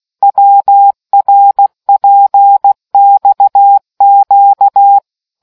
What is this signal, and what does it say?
Morse code, heard as a keyed tone, about 16 wpm, 800 Hz. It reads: WRPXQ